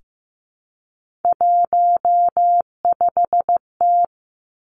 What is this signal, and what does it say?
Morse code, heard as a keyed tone, about 15 wpm, 700 Hz. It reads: 15T
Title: Three Challenging Letter-Number Combinations 15wpm